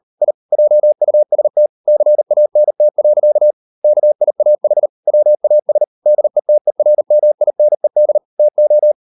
Morse code, heard as a keyed tone, about 38 wpm, 600 Hz. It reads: I JUST CANT. KIAH WAS DETERMINED TO